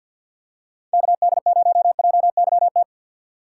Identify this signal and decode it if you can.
Morse code, heard as a keyed tone, about 47 wpm, 690 Hz. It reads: KD0JYT
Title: USA Call Signs - 2x Character Spacing - 50wpm